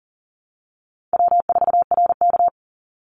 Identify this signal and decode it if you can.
Morse code, heard as a keyed tone, about 40 wpm, 700 Hz. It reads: W4RK
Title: NA SOTA Chaser Callsigns 40wpm